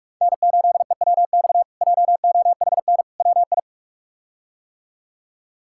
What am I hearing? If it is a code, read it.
Morse code, heard as a keyed tone, about 46 wpm, 700 Hz. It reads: N8EWX JOHN WI
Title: K1USN SST QSO Practice - 45wpm